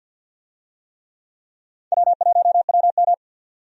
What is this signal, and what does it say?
Morse code, heard as a keyed tone, about 47 wpm, 700 Hz. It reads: W1WM
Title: USA Call Signs - 4 Characters - 50wpm